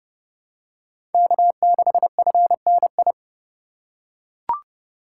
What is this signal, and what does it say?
Morse code, heard as a keyed tone, about 32 wpm, 700 Hz. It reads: K6FNI
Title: USA Call Signs - 3x Character Spacing - 30wpm